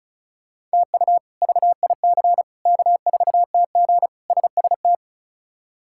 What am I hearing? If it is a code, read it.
Morse code, heard as a keyed tone, about 36 wpm, 700 Hz. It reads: TU VIC K4TG SST
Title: K1USN SST QSO Practice - 35wpm